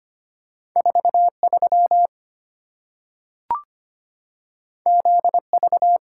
Morse code, heard as a keyed tone, about 25 wpm, 700 Hz. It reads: 43 ZV
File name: Two Letter-Number Combinations - 25wpm